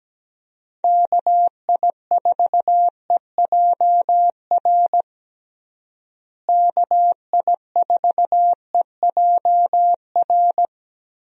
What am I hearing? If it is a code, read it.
Morse code, heard as a keyed tone, about 17 wpm, 700 Hz. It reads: KI4EJR KI4EJR
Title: USA Call Signs - 6 Characters - Repeated 3x - 17wpm